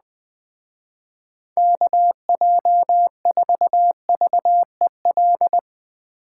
Morse code, heard as a keyed tone, about 19 wpm, 700 Hz. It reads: KJ4VEL
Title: Speed Racing - USA 2x3 Call Signs 17wpm